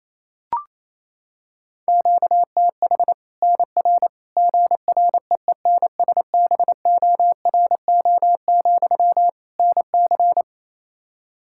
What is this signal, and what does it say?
Morse code, heard as a keyed tone, about 28 wpm, 700 Hz. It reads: QTH NR GREENSBORO, NC
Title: Practice QSO - Exchange 1 - Beginning - 28wpm